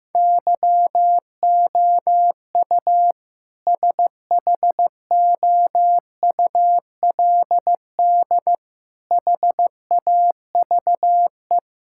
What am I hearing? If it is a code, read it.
Morse code, heard as a keyed tone, about 15 wpm, 700 Hz. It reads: YOU SHOULD HAVE
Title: Sentences from Top 1600 Words 15wpm